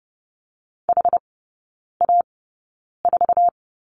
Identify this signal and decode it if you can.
Morse code, heard as a keyed tone, about 30 wpm, spaced as for 9 wpm, 700 Hz. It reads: HA4
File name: 081 Lesson 14e - Number 2 - Three Characters 30wpm